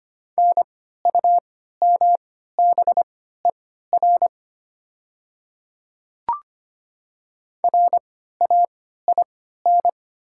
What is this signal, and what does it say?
Morse code, heard as a keyed tone, about 25 wpm, spaced as for 14 wpm, 690 Hz. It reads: NUMBER RAIN
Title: 3x Character Spacing - Top 300 Words - 25wpm